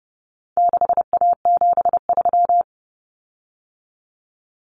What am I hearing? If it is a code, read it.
Morse code, heard as a keyed tone, about 30 wpm, 700 Hz. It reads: 6A73
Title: Common Cut Numbers - 4-Digit 30wpm